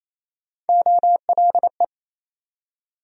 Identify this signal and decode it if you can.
Morse code, heard as a keyed tone, about 28 wpm, 700 Hz. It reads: OLE
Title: USA Call Sign Postfixes - 3 Letters 28wpm